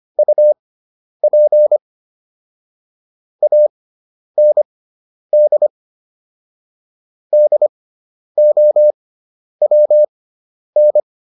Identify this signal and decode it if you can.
Morse code, heard as a keyed tone, about 25 wpm, spaced as for 10 wpm, 600 Hz. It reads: UP AND DOWN